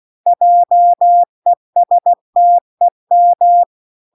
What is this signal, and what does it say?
Morse code, heard as a keyed tone, about 16 wpm, 700 Hz. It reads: JESTEM